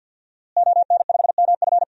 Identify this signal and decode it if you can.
Morse code, heard as a keyed tone, about 50 wpm, 700 Hz. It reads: ON5MF
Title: DXCC-100 Call Signs - 5 Characters - Repeated 3x - 50wpm